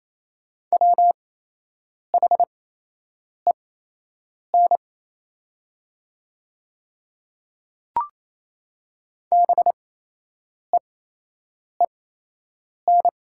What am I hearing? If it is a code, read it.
Morse code, heard as a keyed tone, about 27 wpm, spaced as for 8 wpm, 700 Hz. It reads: WHEN BEEN
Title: 8x Character Spacing - Top 200 Words - 28wpm